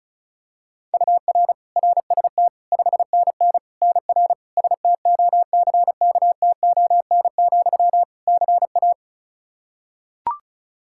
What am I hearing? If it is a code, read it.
Morse code, heard as a keyed tone, about 36 wpm, 700 Hz. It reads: UR RST 5NN NR STOCKTON, CA
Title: Practice QSO - Exchange 1 - Beginning - 35wpm